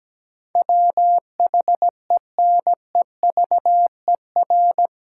Morse code, heard as a keyed tone, about 17 wpm, 700 Hz. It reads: WHENEVER
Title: Sentences from Top 1200 Words - New Word Drill - 17wpm